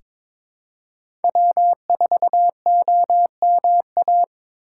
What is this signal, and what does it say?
Morse code, heard as a keyed tone, about 22 wpm, 700 Hz. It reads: W4OMA
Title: USA Call Signs - 5 Characters - 3x Character Spacing - 22wpm